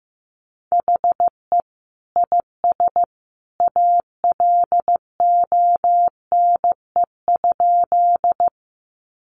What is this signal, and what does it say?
Morse code, heard as a keyed tone, about 15 wpm, 700 Hz. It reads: HE IS ALONE?
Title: Sentences from Top 1000 Words 15wpm